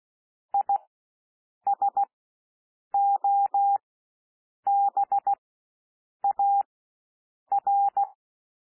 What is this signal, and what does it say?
Morse code, heard as a keyed tone, about 16 wpm, spaced as for 7 wpm, 800 Hz. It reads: ISOBAR
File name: cwmp3 / W1 SOUND WORDS 16 wpm